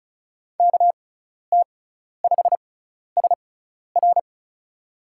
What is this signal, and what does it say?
Morse code, heard as a keyed tone, about 36 wpm, spaced as for 12 wpm, 700 Hz. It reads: KT5SR
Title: USA Call Signs - 5 Characters - 6x Character Spacing - 35wpm